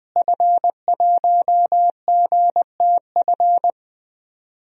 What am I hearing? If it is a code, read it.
Morse code, heard as a keyed tone, about 20 wpm, 700 Hz. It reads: F1GTF
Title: DXCC-100 Call Signs - 5 Characters - Repeated 2x - 20wpm